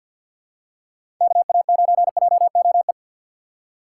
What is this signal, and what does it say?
Morse code, heard as a keyed tone, about 48 wpm, 690 Hz. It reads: KA9JOE
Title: USA Call Signs - 2x Character Spacing - 50wpm